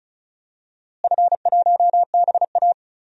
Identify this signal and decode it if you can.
Morse code, heard as a keyed tone, about 36 wpm, 700 Hz. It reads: F1BA
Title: DXCC-100 Call Signs - 4 Characters - Repeated 3x - 35wpm